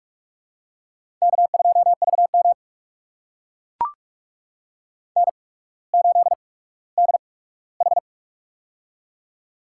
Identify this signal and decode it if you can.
Morse code, heard as a keyed tone, about 44 wpm, spaced as for 19 wpm, 700 Hz. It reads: K2VM N8DH
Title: USA Call Signs - 4 Characters - 8x Character Spacing - 45wpm